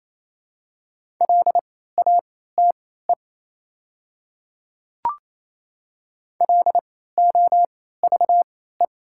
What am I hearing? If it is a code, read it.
Morse code, heard as a keyed tone, about 28 wpm, spaced as for 16 wpm, 690 Hz. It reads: LATE LOVE
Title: 3x Character Spacing - Top 500 Words - 28wpm